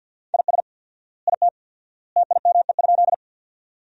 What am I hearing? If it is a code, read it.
Morse code, heard as a keyed tone, about 47 wpm, 700 Hz. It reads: IS IT TIME?
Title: 4x Word-Spacing - Sentences from Top 100 Words - 50wpm